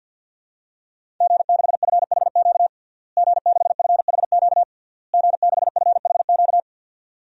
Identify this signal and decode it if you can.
Morse code, heard as a keyed tone, about 46 wpm, 700 Hz. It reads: G6FHQ G6FHQ G6FHQ
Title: DXCC-100 Call Signs - 5 Characters - Repeated 3x - 50wpm